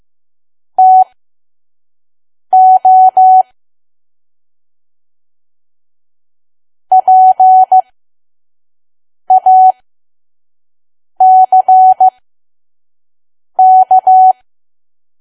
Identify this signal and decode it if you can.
Morse code, heard as a keyed tone, about 15 wpm, spaced as for 5 wpm, 750 Hz. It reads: TO PACK
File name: Archive / 5 WPM files / PM